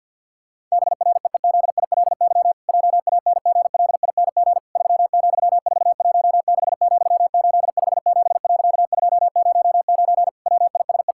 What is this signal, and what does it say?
Morse code, heard as a keyed tone, about 50 wpm, 700 Hz. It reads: BREEZILY JANGLING 3,416,857.209 WISE
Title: Pangram Challenge v2 - 50wpm